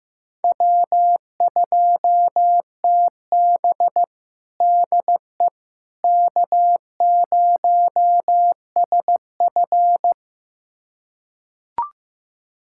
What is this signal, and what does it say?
Morse code, heard as a keyed tone, about 15 wpm, 700 Hz. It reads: W2TB DE K0SF
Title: USA Callsign DE Callsign - 4 Characters - 15wpm